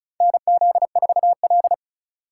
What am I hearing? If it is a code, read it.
Morse code, heard as a keyed tone, about 36 wpm, 700 Hz. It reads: NZ4L
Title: USA Call Signs - 4 Characters - 4x Character Spacing - 35wpm